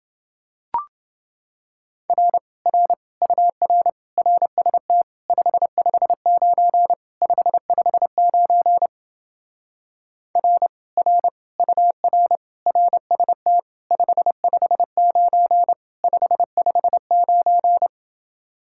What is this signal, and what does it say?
Morse code, heard as a keyed tone, about 30 wpm, 700 Hz. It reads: R R UR RST 559 559 R R UR RST 559 559